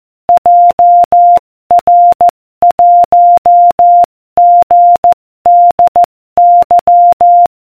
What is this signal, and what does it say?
Morse code, heard as a keyed tone, about 14 wpm, 700 Hz. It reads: JR1GDY